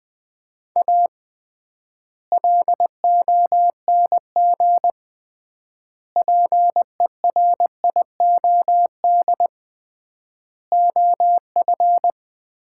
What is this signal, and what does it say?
Morse code, heard as a keyed tone, about 20 wpm, 700 Hz. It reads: A LONG PERIOD OF
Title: 3x Word-Spacing - Sets of 5 Words - 20wpm